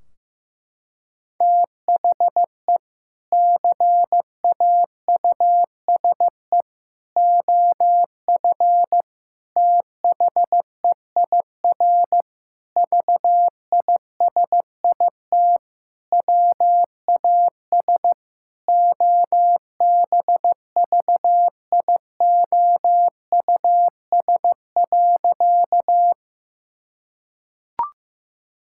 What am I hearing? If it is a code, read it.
Morse code, heard as a keyed tone, about 15 wpm, 700 Hz. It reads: THE CAUSE OF THEIR VISIT WAS OBVIOUS.
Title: Sentences from Top 1600 Words 15wpm